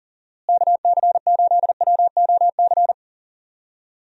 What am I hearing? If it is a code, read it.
Morse code, heard as a keyed tone, about 39 wpm, 700 Hz. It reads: KC8WOC
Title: USA Call Signs - 6 Characters - Repeated 2x - 40wpm